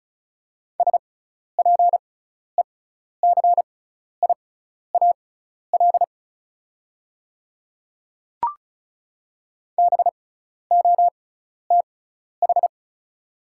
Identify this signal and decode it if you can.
Morse code, heard as a keyed tone, about 34 wpm, spaced as for 12 wpm, 700 Hz. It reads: SPECIAL BOTH
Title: 6x Character Spacing - Top 500 Words - 35wpm